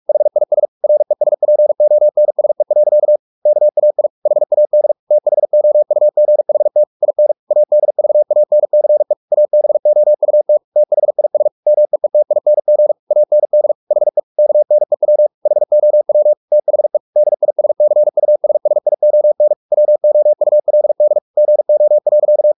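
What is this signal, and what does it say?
Morse code, heard as a keyed tone, about 44 wpm, 600 Hz. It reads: HIS RESPONSE. KAI HAD THOUGHT IN ADVANCE ABOUT THIS MEETING AND HE KNEW HOW THE DISCUSSION WOULD GO.
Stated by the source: Morse practice at 45 wpm, the novel Never by Ken Follett in English, from CW med Gnister.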